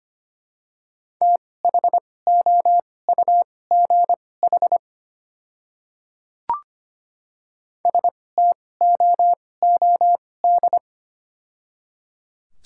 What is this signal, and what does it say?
Morse code, heard as a keyed tone, about 25 wpm, spaced as for 18 wpm, 700 Hz. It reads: THOUGH STOOD
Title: 2x Character Spacing - Top 500 Words - 25wpm